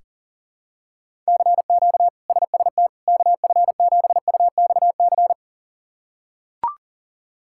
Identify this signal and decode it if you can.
Morse code, heard as a keyed tone, about 40 wpm, 700 Hz. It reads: CQ SST KF7UXC
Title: K1USN SST QSO Practice - 40wpm